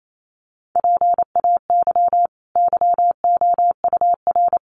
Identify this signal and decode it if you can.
Morse code, heard as a keyed tone, about 28 wpm, 700 Hz. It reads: PAY YOUR
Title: Sentences from Top 900 Words 28wpm